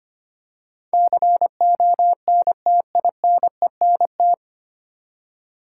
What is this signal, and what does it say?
Morse code, heard as a keyed tone, about 25 wpm, 700 Hz. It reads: CONTINENT
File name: Nine Letter Words - 25wpm